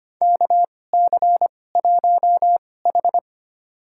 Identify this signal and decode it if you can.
Morse code, heard as a keyed tone, about 25 wpm, spaced as for 18 wpm, 700 Hz. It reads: KC1H
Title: USA Call Signs - 4 Characters - 2x Character Spacing - 25wpm